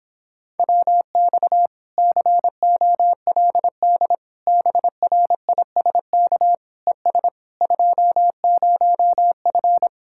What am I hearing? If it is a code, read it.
Morse code, heard as a keyed tone, about 26 wpm, 700 Hz. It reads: WX COLD BRISK ES 20F